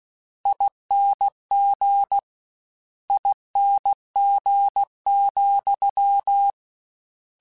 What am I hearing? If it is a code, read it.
Morse code, heard as a keyed tone, about 16 wpm, 800 Hz. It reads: ING ING,